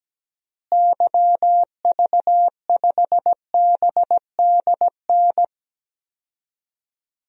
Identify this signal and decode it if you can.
Morse code, heard as a keyed tone, about 17 wpm, 700 Hz. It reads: YV5BDN
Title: DXCC-100 Call Signs - 6 Characters - Repeated 2x - 17wpm